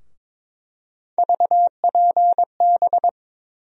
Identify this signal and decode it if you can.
Morse code, heard as a keyed tone, about 23 wpm, 700 Hz. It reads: VPB
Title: USA Call Sign Postfixes - 3 Letters 22wpm